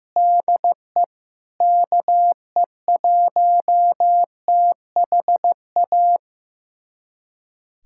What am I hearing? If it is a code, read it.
Morse code, heard as a keyed tone, about 15 wpm, 700 Hz. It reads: DE KE1THA
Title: USA Callsign DE Callsign 15wpm